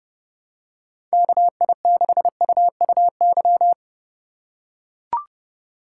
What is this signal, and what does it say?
Morse code, heard as a keyed tone, about 30 wpm, 700 Hz. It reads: KI6UUY
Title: USA Call Signs - 6 Characters - 2x Character Spacing - 30wpm